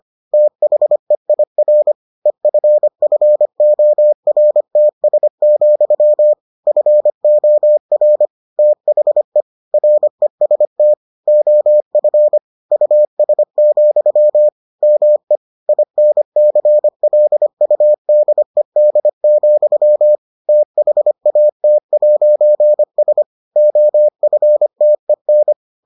Morse code, heard as a keyed tone, about 25 wpm, 600 Hz. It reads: THEIR EFFORTS, FOR THE REST OF US, ME INCLUDED, THAT'S OFTEN